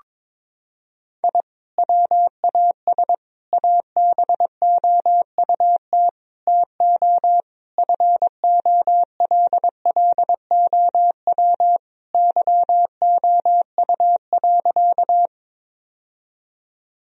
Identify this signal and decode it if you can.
Morse code, heard as a keyed tone, about 22 wpm, 700 Hz. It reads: I WAS ABOUT TO FOLLOW YOU.
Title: Sentences from Top 1000 Words - Review Part 1 - 22wpm